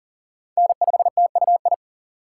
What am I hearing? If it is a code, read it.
Morse code, heard as a keyed tone, about 37 wpm, 700 Hz. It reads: N5TUI
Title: USA Call Signs - 5 Characters - 4x Character Spacing - 40wpm